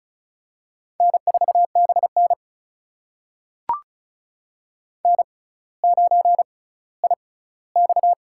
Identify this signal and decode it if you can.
Morse code, heard as a keyed tone, about 36 wpm, spaced as for 18 wpm, 700 Hz. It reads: N4BN N9IX
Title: USA Call Signs - 4 Characters - 6x Character Spacing - 35wpm